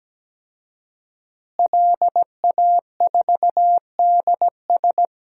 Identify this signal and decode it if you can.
Morse code, heard as a keyed tone, about 17 wpm, 700 Hz. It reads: LA4DS
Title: DXCC-100 Call Signs - 5 Characters - Repeated 3x - 17wpm